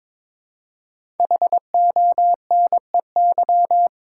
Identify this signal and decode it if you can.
Morse code, heard as a keyed tone, about 22 wpm, 700 Hz. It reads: HONEY
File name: 5 Letter Words - First 3 Letters Given - 22wpm